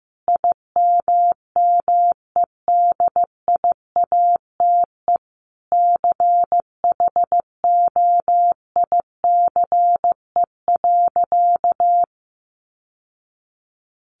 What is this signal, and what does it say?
Morse code, heard as a keyed tone, about 14 wpm, 700 Hz. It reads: IMMEDIATE CHOICE.
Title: Sentences from Top 1200 Words 15wpm